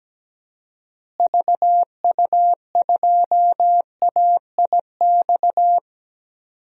Morse code, heard as a keyed tone, about 17 wpm, 700 Hz. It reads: VU2AIX
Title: DXCC-100 Call Signs - 6 Characters - Repeated 2x - 17wpm